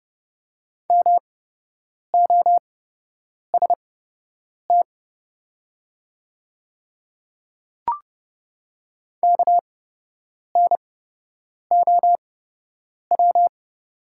Morse code, heard as a keyed tone, about 30 wpm, spaced as for 8 wpm, 700 Hz. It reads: MOST KNOW
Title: 8x Character Spacing - Top 100 Words - 30wpm